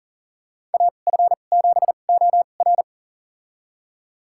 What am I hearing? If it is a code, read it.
Morse code, heard as a keyed tone, about 40 wpm, spaced as for 29 wpm, 690 Hz. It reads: AF7OR
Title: USA Call Signs - 2x Character Spacing - 40wpm